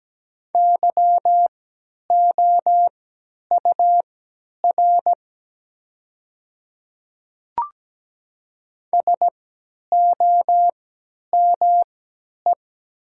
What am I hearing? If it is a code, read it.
Morse code, heard as a keyed tone, about 17 wpm, spaced as for 10 wpm, 700 Hz. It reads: YOUR SOME
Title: 3x Character Spacing - Top 100 Words - 17wpm